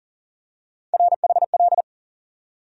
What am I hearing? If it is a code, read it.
Morse code, heard as a keyed tone, about 40 wpm, 700 Hz. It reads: RHL
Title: USA Call Sign Postfixes - 3 Letters 40wpm